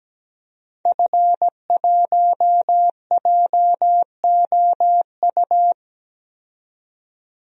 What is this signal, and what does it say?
Morse code, heard as a keyed tone, about 17 wpm, 700 Hz. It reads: F1JOU
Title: DXCC-100 Call Signs - 5 Characters - 17wpm